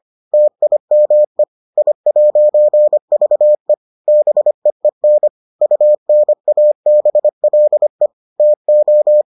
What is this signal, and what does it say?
Morse code, heard as a keyed tone, about 25 wpm, 600 Hz. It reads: TIME I'VE BEEN UNABLE TO